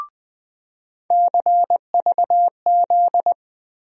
Morse code, heard as a keyed tone, about 20 wpm, 700 Hz. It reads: CVZ
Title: USA Call Sign Postfixes - 3 Letters 20wpm